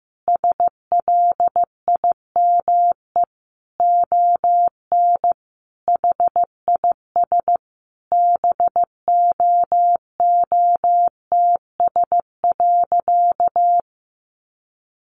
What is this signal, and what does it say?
Morse code, heard as a keyed tone, about 15 wpm, 700 Hz. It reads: SLIME ON HIS BOOTS.
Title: A Sound Of Thunder 15wpm